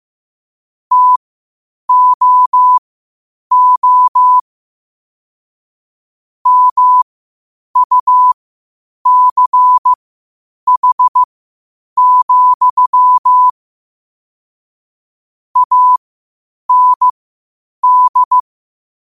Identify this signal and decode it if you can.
Morse code, heard as a keyed tone, about 15 wpm, spaced as for 8 wpm, 1000 Hz. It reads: TOO MUCH, AND
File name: qotdfiles / 8wpm